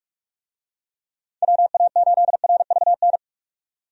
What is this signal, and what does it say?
Morse code, heard as a keyed tone, about 46 wpm, 700 Hz. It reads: WA8RVN
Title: USA Call Signs - 6 Characters - 45wpm